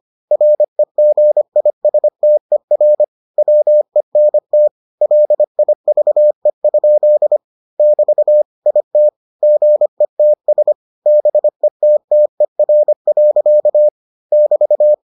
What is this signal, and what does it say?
Morse code, heard as a keyed tone, about 25 wpm, 600 Hz. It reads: REGISTER WENT LIVE? <BT> IT GETS BETTER. <BT>